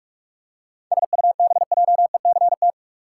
Morse code, heard as a keyed tone, about 46 wpm, 700 Hz. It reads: SUBJECT
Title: Seven Letter Words - 45wpm